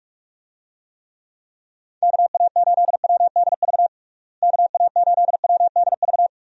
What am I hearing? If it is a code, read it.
Morse code, heard as a keyed tone, about 47 wpm, 700 Hz. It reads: KA8WDV KA8WDV
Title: USA Call Signs - 6 Characters - Repeated 3x - 45wpm